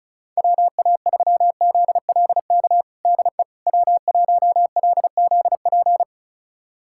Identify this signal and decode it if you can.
Morse code, heard as a keyed tone, about 36 wpm, 700 Hz. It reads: WA3ZLK DE W1LZP
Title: USA Callsign DE Callsign 35wpm